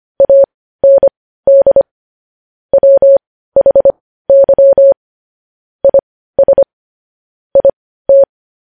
Morse code, heard as a keyed tone, about 25 wpm, spaced as for 15 wpm, 550 Hz. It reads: AND WHY IS IT
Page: news headlines in morse code at 15 wpm